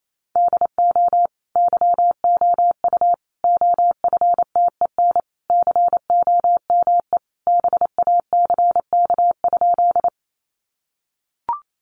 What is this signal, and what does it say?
Morse code, heard as a keyed tone, about 28 wpm, 700 Hz. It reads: DO YOU OFTEN COME BACK?
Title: Sentences from Top 300 Words 28wpm